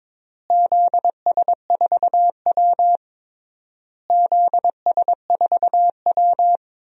Morse code, heard as a keyed tone, about 22 wpm, 700 Hz. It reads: ZS4W ZS4W
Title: DXCC-100 Call Signs - 4 Characters - Repeated 3x - 22wpm